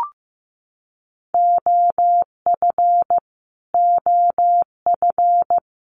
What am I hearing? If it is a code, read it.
Morse code, heard as a keyed tone, about 15 wpm, 700 Hz. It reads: OF OF
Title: Top 200 Words Repeated 15wpm